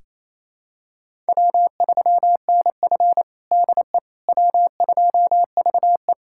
Morse code, heard as a keyed tone, about 28 wpm, 700 Hz. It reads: W3NF DE W2VE
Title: USA Callsign DE Callsign - 4 Characters - 28wpm